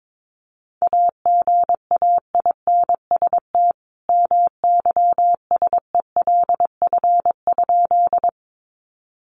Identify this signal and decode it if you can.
Morse code, heard as a keyed tone, about 22 wpm, 700 Hz. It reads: AGAINST MYSELF?
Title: Sentences from Top 1300 Words 22wpm